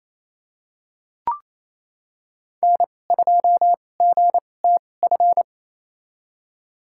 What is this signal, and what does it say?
Morse code, heard as a keyed tone, about 27 wpm, spaced as for 20 wpm, 700 Hz. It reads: N2GTF
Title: USA Call Signs - 5 Characters - 2x Character Spacing - 28wpm